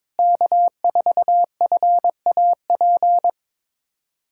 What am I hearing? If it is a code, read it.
Morse code, heard as a keyed tone, about 22 wpm, 700 Hz. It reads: K4FAP
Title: USA Callsign DE Callsign 22wpm